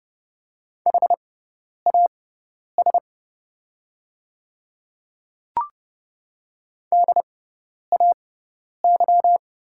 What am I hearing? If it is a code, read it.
Morse code, heard as a keyed tone, about 31 wpm, spaced as for 10 wpm, 700 Hz. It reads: HAS DAY
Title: 6x Character Spacing - Top 100 Words - 30wpm